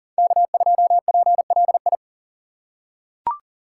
Morse code, heard as a keyed tone, about 40 wpm, 700 Hz. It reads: K2PLI
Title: USA Call Signs - 5 Characters - 2x Character Spacing - 40wpm